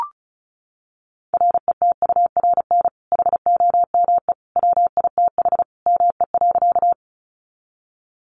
Morse code, heard as a keyed tone, about 34 wpm, 700 Hz. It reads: RETURN HOME WITH ME.
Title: Sentences from Top 800 Words 35wpm